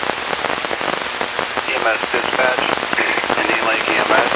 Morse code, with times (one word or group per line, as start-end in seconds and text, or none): none